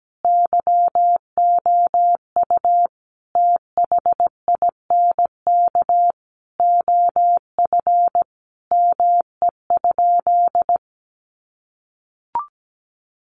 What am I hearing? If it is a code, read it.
Morse code, heard as a keyed tone, about 17 wpm, 700 Hz. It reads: YOU THINK OF ME?
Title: Sentences from Top 1100 Words 17wpm